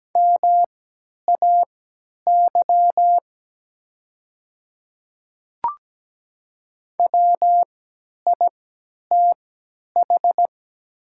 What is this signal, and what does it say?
Morse code, heard as a keyed tone, about 17 wpm, spaced as for 10 wpm, 700 Hz. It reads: MAY WITH
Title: 3x Character Spacing - Top 500 Words - 17wpm